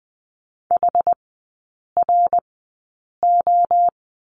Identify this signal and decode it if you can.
Morse code, heard as a keyed tone, about 20 wpm, spaced as for 8 wpm, 700 Hz. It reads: HRO
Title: 114 Lesson 19e - Letter w - Three Characters 20wpm